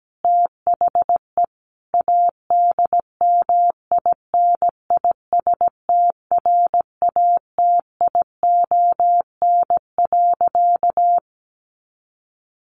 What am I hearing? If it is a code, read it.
Morse code, heard as a keyed tone, about 17 wpm, 700 Hz. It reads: THE ADMINISTRATION.
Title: Sentences from Top 1300 Words 17wpm